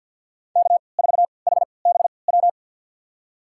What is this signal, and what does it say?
Morse code, heard as a keyed tone, about 51 wpm, spaced as for 28 wpm, 700 Hz. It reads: K4HBW
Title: USA Call Signs - 5 Characters - 3x Character Spacing - 50wpm